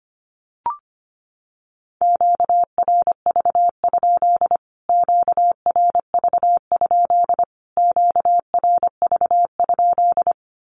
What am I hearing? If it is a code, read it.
Morse code, heard as a keyed tone, about 25 wpm, 700 Hz. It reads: QRV? QRV? QRV?
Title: Q-Codes Repeated 3x 25wpm